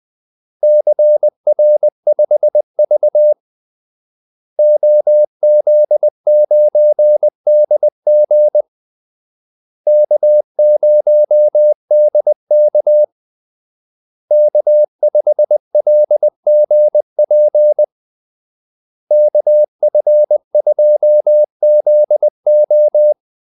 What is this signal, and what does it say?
Morse code, heard as a keyed tone, about 20 wpm, 600 Hz. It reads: CR5V OZ9DG K0DK K5LGP KF2ZO
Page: Kaldesignaler 20 wpm